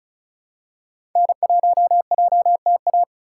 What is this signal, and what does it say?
Morse code, heard as a keyed tone, about 36 wpm, 700 Hz. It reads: N1JTA
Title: USA Call Signs - 5 Characters - 4x Character Spacing - 35wpm